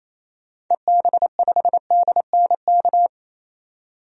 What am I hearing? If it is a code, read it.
Morse code, heard as a keyed tone, about 28 wpm, 700 Hz. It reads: EB5DNK
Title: DXCC-100 Call Signs - 6 Characters - 28wpm